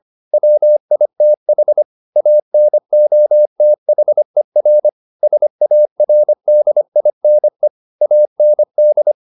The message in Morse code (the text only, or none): WITH ANOTHER SARDINE AND